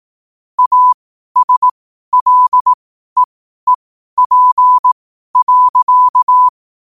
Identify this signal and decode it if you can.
Morse code, heard as a keyed tone, about 18 wpm, spaced as for 12 wpm, 1000 Hz. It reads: ASLEEP.
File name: qotdfiles / 12wpm